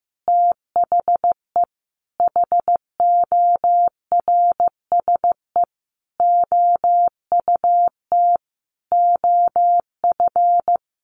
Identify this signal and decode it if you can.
Morse code, heard as a keyed tone, about 15 wpm, 700 Hz. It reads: THE HORSE OUT OF
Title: Sentences from Top 1500 Words 15wpm